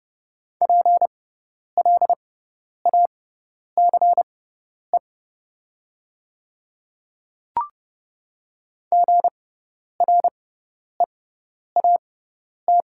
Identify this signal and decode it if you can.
Morse code, heard as a keyed tone, about 30 wpm, spaced as for 10 wpm, 690 Hz. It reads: PLACE GREAT
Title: 6x Character Spacing - Top 500 Words - 30wpm